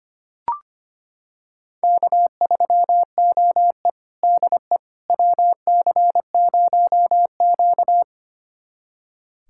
Morse code, heard as a keyed tone, about 25 wpm, 700 Hz. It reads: K3OE DE WC0Q
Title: USA Callsign DE Callsign - 4 Characters - 25wpm